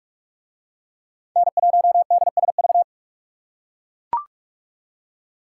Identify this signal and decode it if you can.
Morse code, heard as a keyed tone, about 41 wpm, 700 Hz. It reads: N1DSV